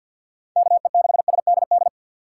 Morse code, heard as a keyed tone, about 48 wpm, 700 Hz. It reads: KE6SDD
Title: USA Call Signs - 6 Characters - 3x Character Spacing - 50wpm